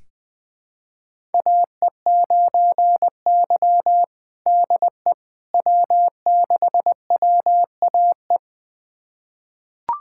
Morse code, heard as a keyed tone, about 20 wpm, 690 Hz. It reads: AE9Y DE W6WAE
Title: USA Callsign DE Callsign 20wpm